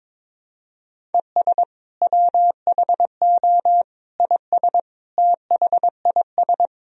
Morse code, heard as a keyed tone, about 22 wpm, 700 Hz. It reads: ES WHO IS THIS
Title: Sentences from Top 300 Words and Abbreviations 22wpm